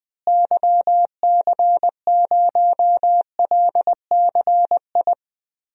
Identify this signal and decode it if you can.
Morse code, heard as a keyed tone, about 20 wpm, 700 Hz. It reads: YC0LCI
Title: DXCC-100 Call Signs - 6 Characters - Repeated 3x - 20wpm